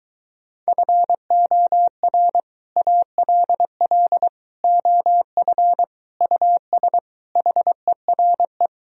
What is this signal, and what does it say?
Morse code, heard as a keyed tone, about 23 wpm, 700 Hz. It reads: FOR ALL OF US HERE